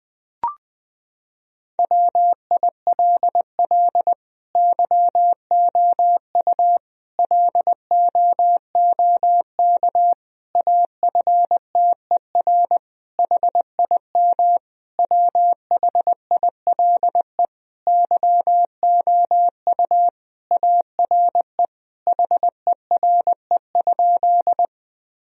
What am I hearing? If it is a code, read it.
Morse code, heard as a keyed tone, about 20 wpm, 700 Hz. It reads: WILL YOU LOOK AFTER HIM WHILE YOU ARE HERE?
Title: Sentences from Top 1000 Words - Review Part 1 - 20wpm